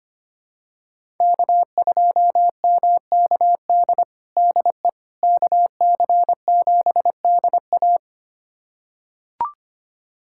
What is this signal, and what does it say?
Morse code, heard as a keyed tone, about 25 wpm, 700 Hz. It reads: K2MKD DE KC7DA
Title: USA Callsign DE Callsign - 5 Characters - 25wpm